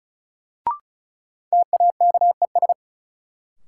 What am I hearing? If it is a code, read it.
Morse code, heard as a keyed tone, about 36 wpm, 700 Hz. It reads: TAKES
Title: Sentences from Top 1000 Words - Review Part 6 - New Word Drill - 35wpm